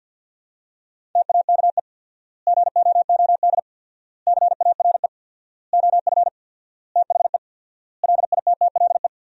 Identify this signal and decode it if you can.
Morse code, heard as a keyed tone, about 50 wpm, 700 Hz. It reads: TAKE GOOD CARE OF THE LITTLE
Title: 4x Word-Spacing - Sentences from Top 400 Words - 50wpm